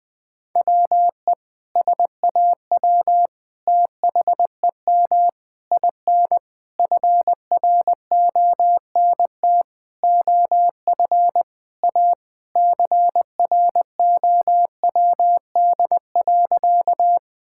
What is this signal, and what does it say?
Morse code, heard as a keyed tone, about 20 wpm, 700 Hz. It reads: WE SAW THEM IN FRONT OF A CROWD.
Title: Sentences with 4-Word Phrases - 20wpm